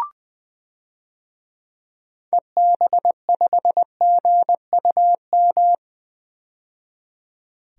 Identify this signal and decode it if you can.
Morse code, heard as a keyed tone, about 20 wpm, 700 Hz. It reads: EB5GUM
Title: DXCC-100 Call Signs - 6 Characters - 20wpm